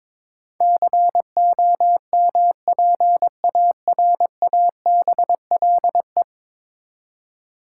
Morse code, heard as a keyed tone, about 22 wpm, 700 Hz. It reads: COMPARABLE
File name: practice / 22wpm / 10 Letter Words - First 3 Letters Given - 22wpm